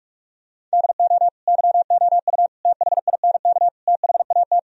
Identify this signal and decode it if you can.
Morse code, heard as a keyed tone, about 49 wpm, 700 Hz. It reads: DO YOU THINK THAT